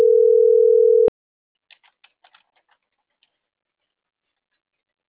Ringback tone (North America), 440 + 480 Hz.